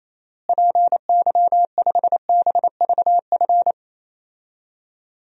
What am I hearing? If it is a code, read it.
Morse code, heard as a keyed tone, about 28 wpm, 700 Hz. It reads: PY5BVF
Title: DXCC-100 Call Signs - 6 Characters - Repeated 3x - 28wpm